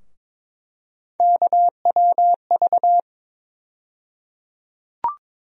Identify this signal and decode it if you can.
Morse code, heard as a keyed tone, about 22 wpm, 700 Hz. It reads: KWV
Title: USA Call Sign Postfixes - 3 Letters 22wpm